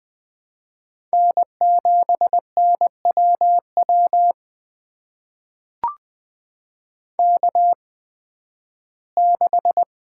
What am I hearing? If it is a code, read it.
Morse code, heard as a keyed tone, about 20 wpm, spaced as for 13 wpm, 700 Hz. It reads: N7NWW K6
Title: USA Call Signs - 5 Characters - 8x Character Spacing - 20wpm